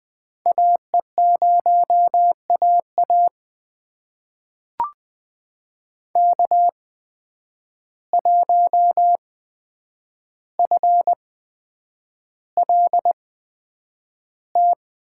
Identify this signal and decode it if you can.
Morse code, heard as a keyed tone, about 20 wpm, spaced as for 9 wpm, 700 Hz. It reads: AE0AA K1FLT
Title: USA Call Signs - 5 Characters - 8x Character Spacing - 20wpm